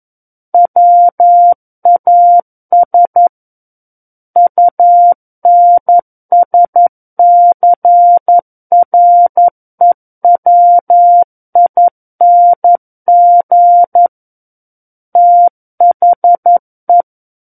Morse code, heard as a keyed tone, about 11 wpm, 700 Hz. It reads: WAS UNSCREWING THE
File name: Book One / 02-Chapter 2 - 11 WPM